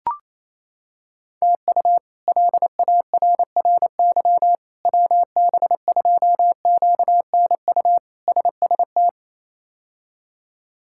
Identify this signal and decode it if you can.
Morse code, heard as a keyed tone, about 28 wpm, 700 Hz. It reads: TU LARRY WB2QNU SST
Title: K1USN SST QSO Practice - 28wpm